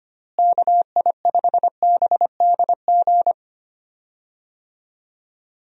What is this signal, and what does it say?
Morse code, heard as a keyed tone, about 25 wpm, 700 Hz. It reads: KI5BDG
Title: USA Call Signs - 6 Characters - Repeated 2x - 25wpm